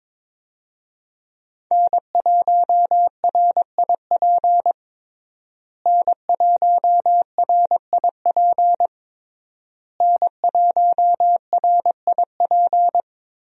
Morse code, heard as a keyed tone, about 21 wpm, 700 Hz. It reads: N1RIP N1RIP N1RIP
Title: USA Call Signs - 5 Characters - Repeated 3x - 22wpm